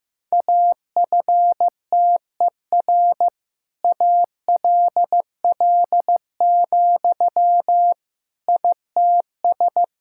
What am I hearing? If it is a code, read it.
Morse code, heard as a keyed tone, about 15 wpm, 700 Hz. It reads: AFTER ALL, ITS
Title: Sentences from Top 1000 Words - Review Part 5 - 15wpm